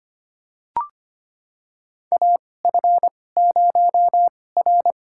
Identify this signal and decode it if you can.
Morse code, heard as a keyed tone, about 25 wpm, spaced as for 18 wpm, 700 Hz. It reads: AF0R
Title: USA Call Signs - 4 Characters - 2x Character Spacing - 25wpm